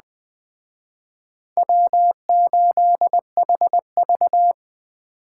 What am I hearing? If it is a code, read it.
Morse code, heard as a keyed tone, about 20 wpm, 700 Hz. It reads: W8HV